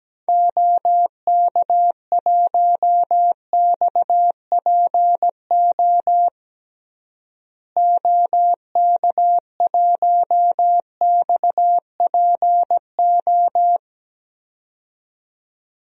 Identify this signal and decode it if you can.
Morse code, heard as a keyed tone, about 17 wpm, 700 Hz. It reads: OK1XPO OK1XPO
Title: DXCC-100 Call Signs - 6 Characters - Repeated 2x - 17wpm